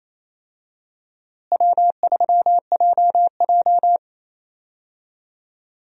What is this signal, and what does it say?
Morse code, heard as a keyed tone, about 28 wpm, 700 Hz. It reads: W3JJ
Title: USA Call Signs - 4 Characters - Repeated 3x - 28wpm